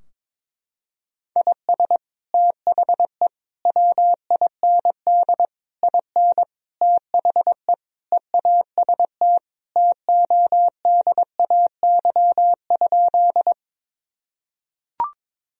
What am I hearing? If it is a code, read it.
Morse code, heard as a keyed tone, about 22 wpm, 700 Hz. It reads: IS THE WIND IN THE EAST TODAY?
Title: Sentences from Top 1000 Words - Review Part 3 - 22wpm